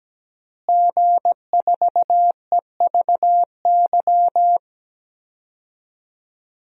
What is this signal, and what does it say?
Morse code, heard as a keyed tone, about 17 wpm, 700 Hz. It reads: G4EVY
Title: DXCC-100 Call Signs - 5 Characters - Repeated 3x - 17wpm